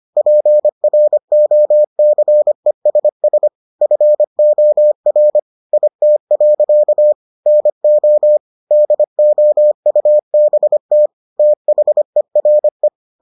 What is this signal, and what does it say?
Morse code, heard as a keyed tone, about 25 wpm, 600 Hz. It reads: PROCESS FOR IT. NO DOUBT THERE